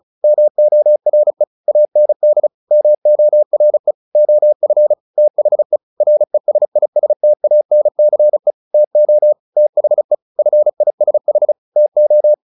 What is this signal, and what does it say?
Morse code, heard as a keyed tone, about 34 wpm, 600 Hz. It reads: MORE AND MORE OF THE RESISTANCE TO THE FISH TO
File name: the old man and the sea 2 35wpm